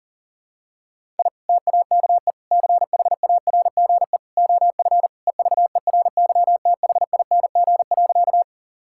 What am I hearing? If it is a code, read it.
Morse code, heard as a keyed tone, about 40 wpm, 700 Hz. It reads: I TAKE CHARGE OF EVERYTHING.
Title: Sentences from Top 1000 Words - Review Part 6 - 40wpm